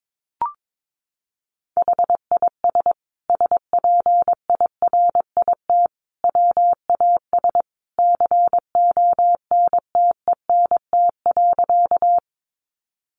Morse code, heard as a keyed tone, about 22 wpm, 700 Hz. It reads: HIS SPIRIT WAS CONTENT.
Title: Sentences from Top 1400 Words 22wpm